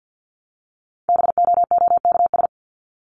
Morse code, heard as a keyed tone, about 50 wpm, 700 Hz. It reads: BOOKS
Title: ES:S Suffix - Top 100 Words with Base Words 50wpm